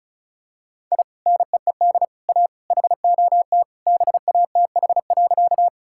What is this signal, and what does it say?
Morse code, heard as a keyed tone, about 36 wpm, 700 Hz. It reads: I NEED A HOT BATH.
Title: Sentences from Top 1600 Words 35wpm